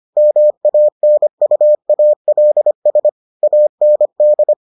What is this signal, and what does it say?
Morse code, heard as a keyed tone, about 25 wpm, 600 Hz. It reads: MANUALS AND